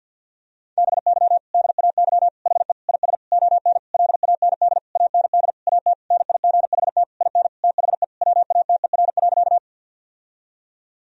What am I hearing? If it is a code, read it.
Morse code, heard as a keyed tone, about 53 wpm, 700 Hz. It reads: BY DAY HE IS ON LAND AND AT NIGHT IN THE WATER.
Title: Sentences from Top 1000 Words - Review Part 1 - 50wpm